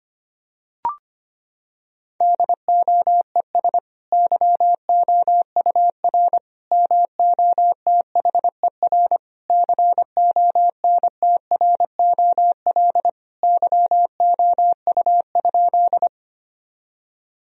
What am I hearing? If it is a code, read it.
Morse code, heard as a keyed tone, about 25 wpm, 700 Hz. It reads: DOES YOUR MOTHER CONTROL YOU?
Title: Sentences from Top 1000 Words - Review Part 3 - 25wpm